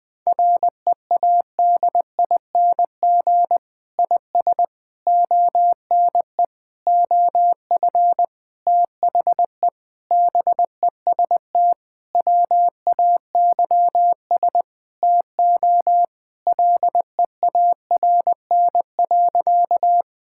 Morse code, heard as a keyed tone, about 20 wpm, 700 Hz. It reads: READING IS ONE OF THE BEST WAYS TO LEARN.
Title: Sentences with 6-Word Phrases - 20wpm